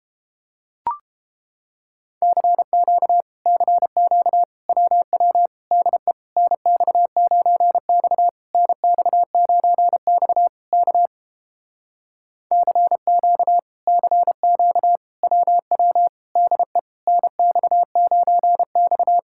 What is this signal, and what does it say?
Morse code, heard as a keyed tone, about 30 wpm, 700 Hz. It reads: CQ CQ WW DE NX9X NX9X K CQ CQ WW DE NX9X